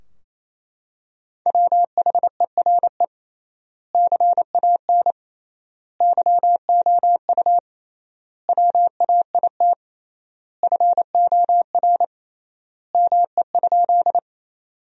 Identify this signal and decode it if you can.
Morse code, heard as a keyed tone, about 28 wpm, 700 Hz. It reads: WHERE CAN YOU WAIT FOR ME?
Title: 3x Word-Spacing - Sentences from Top 500 Words - 28wpm